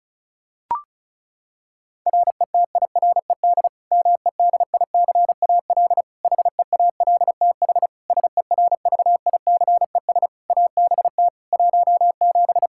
Morse code, heard as a keyed tone, about 36 wpm, 700 Hz. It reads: RETIRED MEDICAL HEALTH SERVICES ABT 17
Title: K1USN SST QSO Practice 35wpm